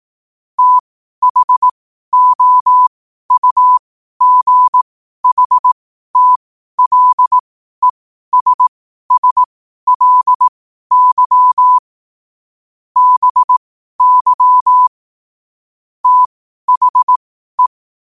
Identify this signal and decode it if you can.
Morse code, heard as a keyed tone, about 18 wpm, spaced as for 12 wpm, 1000 Hz. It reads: THOUGHTLESSLY BY THE
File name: qotdfiles / 12wpm